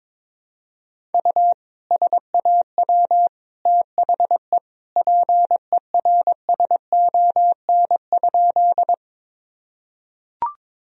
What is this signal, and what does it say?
Morse code, heard as a keyed tone, about 22 wpm, 700 Hz. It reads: U SAW THE PERSON?
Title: Sentences from Top 400 Words and Abbreviations 22wpm